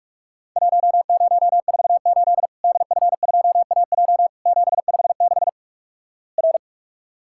Morse code, heard as a keyed tone, about 46 wpm, 700 Hz. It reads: 1048 DF2AJ 756 R
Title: Worked All Europe DX Contest - QTC Exchange - 45wpm